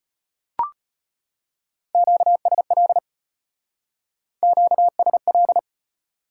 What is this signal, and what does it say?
Morse code, heard as a keyed tone, about 33 wpm, 700 Hz. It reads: QSL QSL